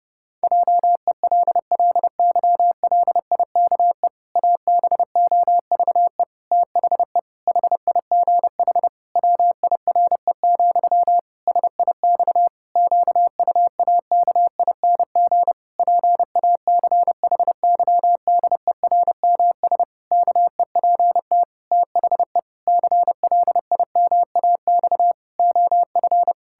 Morse code, heard as a keyed tone, about 30 wpm, 700 Hz. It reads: JELLYLIKE ABOVE THE HIGH WIRE, SIX QUAKING PACHYDERMS KEPT THE CLIMAX OF
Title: Pangram Challenge v2 - 30wpm